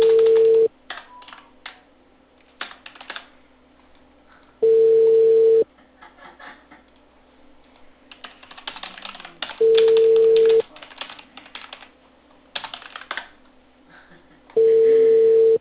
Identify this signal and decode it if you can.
Ringback tone (China, 450 Hz).